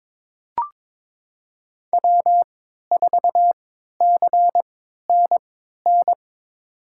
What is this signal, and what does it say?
Morse code, heard as a keyed tone, about 22 wpm, spaced as for 12 wpm, 700 Hz. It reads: W4CNN
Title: USA Call Signs - 5 Characters - 3x Character Spacing - 22wpm